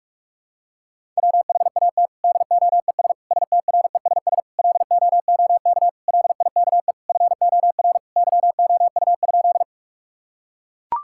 Morse code, heard as a keyed tone, about 42 wpm, 700 Hz. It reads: WHAT DOES STRESS LOOK LIKE FOR YOU?
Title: Sentences from Top 1000 Words - Review Part 6 - 45wpm